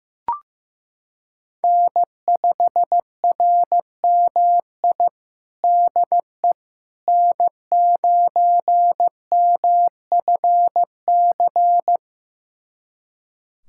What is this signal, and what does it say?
Morse code, heard as a keyed tone, about 15 wpm, 700 Hz. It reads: N5RMI DE N9MFC
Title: USA Callsign DE Callsign - 5 Characters - 15wpm